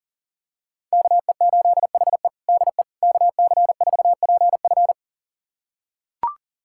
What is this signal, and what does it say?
Morse code, heard as a keyed tone, about 40 wpm, 700 Hz. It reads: KE8HE DE KC4PF
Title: USA Callsign DE Callsign - 5 Characters - 40wpm